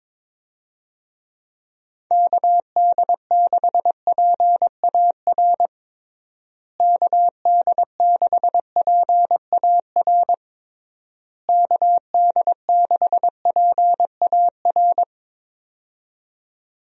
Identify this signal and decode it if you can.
Morse code, heard as a keyed tone, about 22 wpm, 700 Hz. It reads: KD6PAR KD6PAR KD6PAR
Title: USA Call Signs - 6 Characters - Repeated 3x - 22wpm